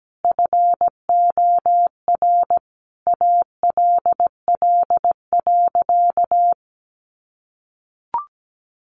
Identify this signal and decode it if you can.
Morse code, heard as a keyed tone, about 17 wpm, 700 Hz. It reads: FOR ALL.
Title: Sentences from Top 1500 Words 17wpm